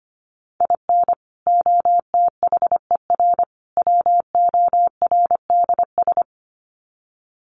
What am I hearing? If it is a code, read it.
Morse code, heard as a keyed tone, about 25 wpm, 700 Hz. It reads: IN OTHER WORDS
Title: Sets of 3 Words 25wpm